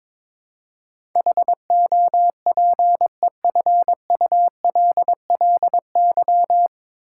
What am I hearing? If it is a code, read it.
Morse code, heard as a keyed tone, about 22 wpm, 700 Hz. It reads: HOPEFULLY
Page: Morse Code Ninja practice 22wpm